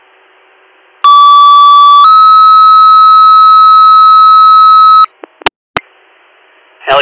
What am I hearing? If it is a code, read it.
Two-tone page, heard as a one-second tone, then a three-second tone, about 1150 then 1290 Hz. It is heard from 1.0 to 5.1 s.